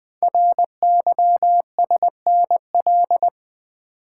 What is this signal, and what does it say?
Morse code, heard as a keyed tone, about 20 wpm, 700 Hz. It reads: RYSNL
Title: Code Groups - 5 Letters - 20wpm